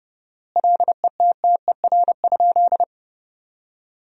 Morse code, heard as a keyed tone, about 30 wpm, 700 Hz. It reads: LETTER?
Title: 4x Word-Spacing - Sentences from Top 300 Words - 30wpm